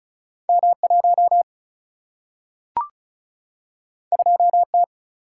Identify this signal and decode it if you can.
Morse code, heard as a keyed tone, about 36 wpm, 700 Hz. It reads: M1 2T
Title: Single-Letter Single-Number 35wpm